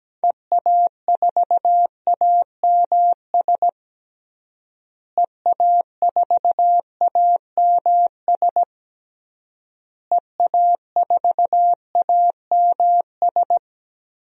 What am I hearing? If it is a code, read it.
Morse code, heard as a keyed tone, about 17 wpm, 700 Hz. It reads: EA4AMS EA4AMS EA4AMS
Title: DXCC-100 Call Signs - 6 Characters - Repeated 3x - 17wpm